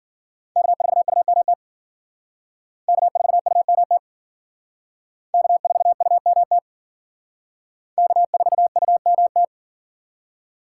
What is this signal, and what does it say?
Morse code, heard as a keyed tone, about 50 wpm, 700 Hz. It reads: K4UMT K4UMT K4UMT K4UMT
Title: Speed Racing - USA 1x3 Call Signs 40wpm